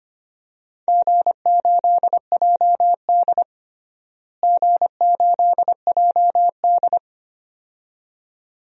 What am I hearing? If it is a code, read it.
Morse code, heard as a keyed tone, about 25 wpm, 700 Hz. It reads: G8JD G8JD
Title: DXCC-100 Call Signs - 4 Characters - Repeated 2x - 25wpm